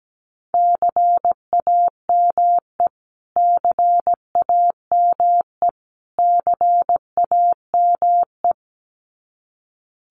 Morse code, heard as a keyed tone, about 17 wpm, 700 Hz. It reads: CAME CAME CAME
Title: Top 500 Words Repeated 17wpm